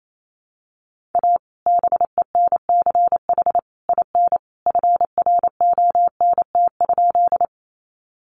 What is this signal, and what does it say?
Morse code, heard as a keyed tone, about 28 wpm, 700 Hz. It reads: A BENCH IN FRONT?
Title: Sentences from Top 1400 Words 28wpm